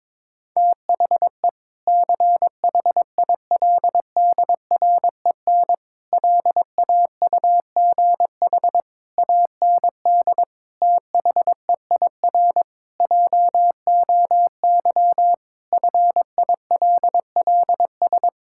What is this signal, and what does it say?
Morse code, heard as a keyed tone, about 22 wpm, 700 Hz. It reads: THE CHILDREN LAUGH AND THEIR JOY FILLS
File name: practice / 22wpm / Sentences with 2-Word Phrases - 22wpm